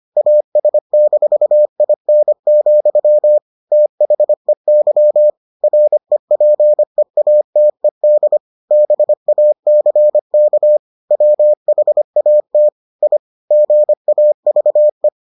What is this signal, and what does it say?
Morse code, heard as a keyed tone, about 25 wpm, 600 Hz. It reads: AS-IN, THEY REPEATED BACK WHAT I GAVE